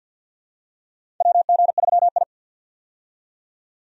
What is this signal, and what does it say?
Morse code, heard as a keyed tone, about 47 wpm, 700 Hz. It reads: WG3I
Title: USA Call Signs - 4 Characters - 50wpm